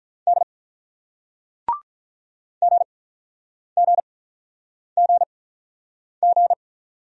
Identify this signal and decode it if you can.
Morse code, heard as a keyed tone, about 48 wpm, spaced as for 9 wpm, 700 Hz. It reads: D GGGG